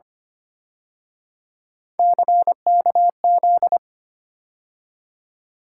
Morse code, heard as a keyed tone, about 29 wpm, 700 Hz. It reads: CKZ
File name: Speed-Racing - Three Letter-Number Combinations - 25wpm